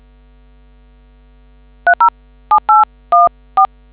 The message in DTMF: 307814